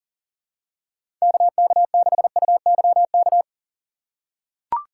Morse code, heard as a keyed tone, about 39 wpm, 700 Hz. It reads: KK6UYK
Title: USA Call Signs - 6 Characters - 2x Character Spacing - 40wpm